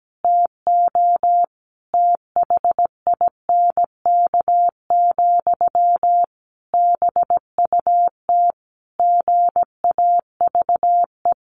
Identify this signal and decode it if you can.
Morse code, heard as a keyed tone, about 17 wpm, 700 Hz. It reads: TO THINK, BUT GAVE
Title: A Sound Of Thunder 17wpm